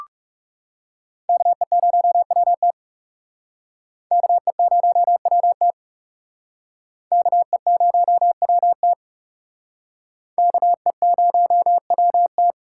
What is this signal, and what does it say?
Morse code, heard as a keyed tone, about 38 wpm, 700 Hz. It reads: KE0WT KE0WT KE0WT KE0WT